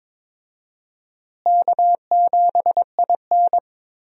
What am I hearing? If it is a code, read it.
Morse code, heard as a keyed tone, about 22 wpm, 700 Hz. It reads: K7IN
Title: USA Call Signs - 4 Characters - Repeated 2x - 22wpm